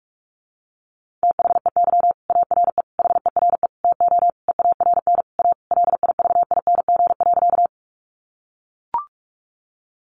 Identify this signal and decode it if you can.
Morse code, heard as a keyed tone, about 44 wpm, 700 Hz. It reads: THEY ARE HERE TO EARN A LIVING.
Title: Sentences from Top 1500 Words 45wpm